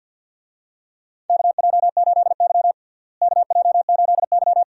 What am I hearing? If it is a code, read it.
Morse code, heard as a keyed tone, about 46 wpm, 700 Hz. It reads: KJ8Y KJ8Y
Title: USA Call Signs - 4 Characters - Repeated 2x - 50wpm